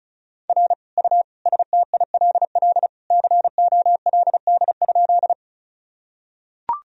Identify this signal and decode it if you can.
Morse code, heard as a keyed tone, about 36 wpm, 700 Hz. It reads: R U STILL COLD?
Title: Sentences from Top 400 Words and Abbreviations 35wpm